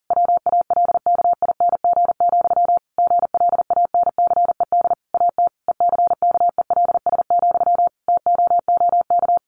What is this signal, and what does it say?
Morse code, heard as a keyed tone, about 40 wpm, 700 Hz. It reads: WALKING, GLANCED AT ECKELS, TOOK
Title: A Sound Of Thunder 40wpm